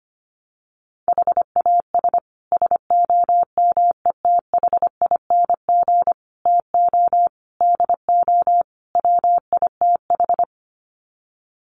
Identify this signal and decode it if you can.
Morse code, heard as a keyed tone, about 25 wpm, 700 Hz. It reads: HAS SOMETHING TO DO WITH
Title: Sets of 5 Words Encore 25wpm